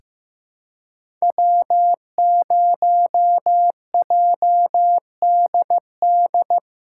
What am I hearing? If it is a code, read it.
Morse code, heard as a keyed tone, about 15 wpm, 700 Hz. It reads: W0JDD
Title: USA Callsign DE Callsign 15wpm